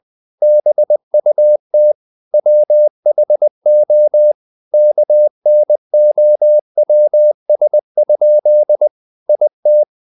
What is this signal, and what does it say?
Morse code, heard as a keyed tone, about 20 wpm, 600 Hz. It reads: BUT WHO KNOWS? IT
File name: the old man and the sea part 2 20wpm